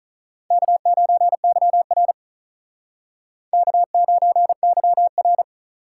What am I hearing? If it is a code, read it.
Morse code, heard as a keyed tone, about 47 wpm, 700 Hz. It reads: K9YR K9YR